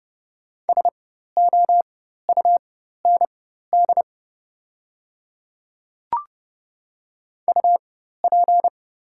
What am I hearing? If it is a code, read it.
Morse code, heard as a keyed tone, about 30 wpm, spaced as for 14 wpm, 700 Hz. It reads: SOUND UP
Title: 4x Character Spacing - Top 100 Words - 30wpm